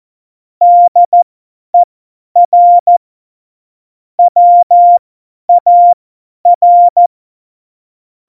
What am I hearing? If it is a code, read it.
Morse code, heard as a keyed tone, about 14 wpm, spaced as for 10 wpm, 700 Hz. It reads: DER WAR
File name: Morsekurs / Witze / Witz 19. November 2019, 50 F70